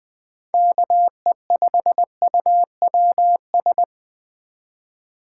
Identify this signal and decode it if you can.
Morse code, heard as a keyed tone, about 20 wpm, 700 Hz. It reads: KE5UWS
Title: Speed Racing - USA 2x3 Call Signs 17wpm